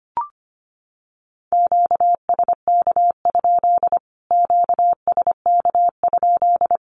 Morse code, heard as a keyed tone, about 25 wpm, 700 Hz. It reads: QSK? QSK?
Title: Q-Codes Repeated 3x 25wpm